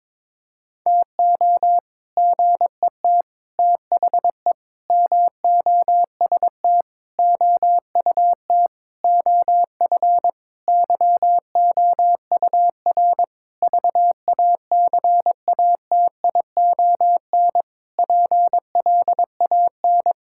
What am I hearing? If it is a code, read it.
Morse code, heard as a keyed tone, about 22 wpm, 700 Hz. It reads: TO GET THE MOST OUT OF YOUR VACATION PLAN
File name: Sentences with 7-Word Phrases - 22wpm